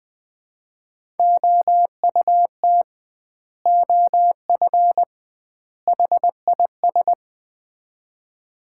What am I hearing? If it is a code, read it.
Morse code, heard as a keyed tone, about 20 wpm, 700 Hz. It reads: OUT OF HIS
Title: 2x Word-Spacing - Sets of 3 Words - 20wpm